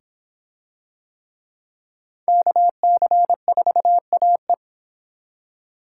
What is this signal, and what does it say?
Morse code, heard as a keyed tone, about 23 wpm, 700 Hz. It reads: KC4AE